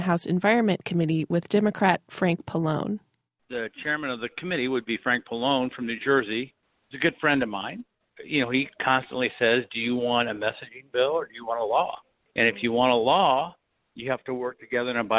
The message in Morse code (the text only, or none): none